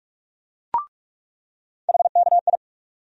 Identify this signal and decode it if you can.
Morse code, heard as a keyed tone, about 46 wpm, 700 Hz. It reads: HKI
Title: USA Call Sign Postfixes - 3 Letters 45wpm